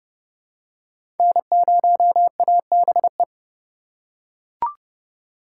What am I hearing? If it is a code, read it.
Morse code, heard as a keyed tone, about 29 wpm, 700 Hz. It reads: N0ABE
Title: USA Call Signs - 5 Characters - 3x Character Spacing - 30wpm